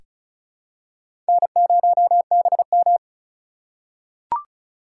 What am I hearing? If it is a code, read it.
Morse code, heard as a keyed tone, about 36 wpm, 700 Hz. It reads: N0BM
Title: USA Call Signs - 4 Characters - 3x Character Spacing - 35wpm